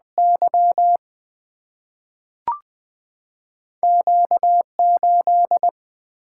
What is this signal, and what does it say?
Morse code, heard as a keyed tone, about 20 wpm, 700 Hz. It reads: Y Q 8
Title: Two Letter-Number Combinations - 20wpm